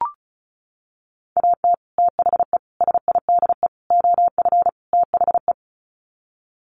Morse code, heard as a keyed tone, about 34 wpm, 700 Hz. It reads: AT THE SIDE OF THE
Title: Sets of 5 Words Encore2 35wpm